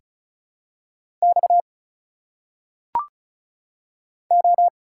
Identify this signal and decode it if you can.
Morse code, heard as a keyed tone, about 36 wpm, 700 Hz. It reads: X O
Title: Single Letters - Phonetic Pronunciation 35wpm